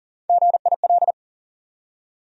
Morse code, heard as a keyed tone, about 40 wpm, 700 Hz. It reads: GIL
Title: USA Call Sign Postfixes - 3 Letters 40wpm